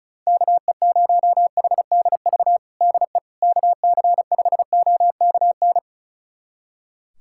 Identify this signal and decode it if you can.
Morse code, heard as a keyed tone, about 36 wpm, 700 Hz. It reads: KE0HDV DE KC5OKN
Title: USA Callsign DE Callsign 35wpm